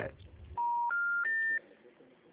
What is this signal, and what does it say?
Special information tone, with speech.